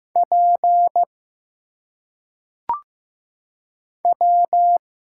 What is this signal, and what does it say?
Morse code, heard as a keyed tone, about 15 wpm, 700 Hz. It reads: P W
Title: Single Letters - Phonetic Pronunciation 15wpm